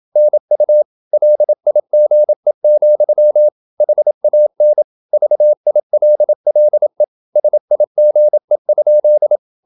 Morse code, heard as a keyed tone, about 27 wpm, 600 Hz. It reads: NU LIGE, HAN VILLE SIGE?
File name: Ferie QRQ 27wpm